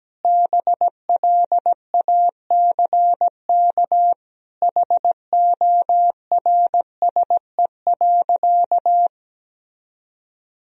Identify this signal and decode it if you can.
Morse code, heard as a keyed tone, about 17 wpm, 700 Hz. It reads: BLACK HORSE.
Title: Sentences from Top 1000 Words - Review Part 2 - 17wpm